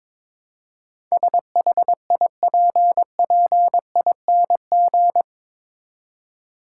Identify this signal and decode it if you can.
Morse code, heard as a keyed tone, about 22 wpm, 700 Hz. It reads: SHIPPING
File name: practice / 22wpm / Eight Letter Words - 22wpm